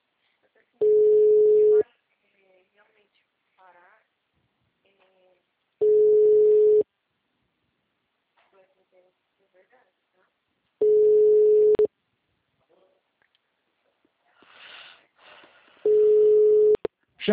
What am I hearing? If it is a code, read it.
Ringback tone (Europe, 425 Hz), with speech.
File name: out- -2013- - - .324